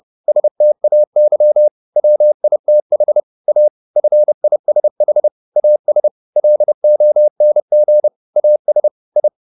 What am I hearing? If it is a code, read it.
Morse code, heard as a keyed tone, about 30 wpm, 600 Hz. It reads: STAY WITH A FISH AS LONG AS I
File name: the old man and the sea / the old man and the sea 1 30wpm